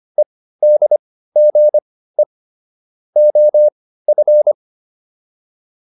Morse code, heard as a keyed tone, about 25 wpm, spaced as for 15 wpm, 600 Hz. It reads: EDGE OF